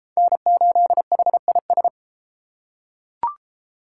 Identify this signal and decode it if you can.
Morse code, heard as a keyed tone, about 31 wpm, 700 Hz. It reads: N8HIS